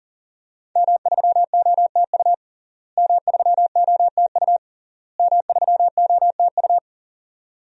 Morse code, heard as a keyed tone, about 40 wpm, 700 Hz. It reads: M3OTU M3OTU M3OTU
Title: DXCC-100 Call Signs - 5 Characters - Repeated 3x - 40wpm